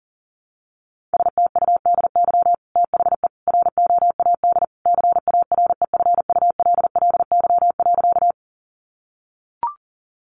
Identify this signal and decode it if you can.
Morse code, heard as a keyed tone, about 40 wpm, 700 Hz. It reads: STUDY THE ROAD CAREFULLY.
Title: Sentences from Top 1300 Words 40wpm